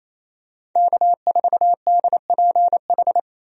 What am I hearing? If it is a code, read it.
Morse code, heard as a keyed tone, about 28 wpm, 700 Hz. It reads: K4DPH
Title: USA Call Signs - 5 Characters - Repeated 3x - 28wpm